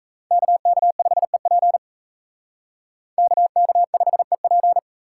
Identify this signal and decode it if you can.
Morse code, heard as a keyed tone, about 36 wpm, 700 Hz. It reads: KK5EP KK5EP